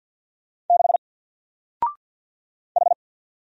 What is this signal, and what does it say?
Morse code, heard as a keyed tone, about 48 wpm, 700 Hz. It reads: 6 H
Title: Single Character (Letter:Number:Punctuation:Prosign) 50wpm